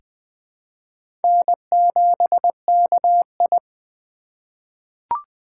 Morse code, heard as a keyed tone, about 20 wpm, 700 Hz. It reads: N7KI
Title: USA Call Signs - 4 Characters - 2x Character Spacing - 20wpm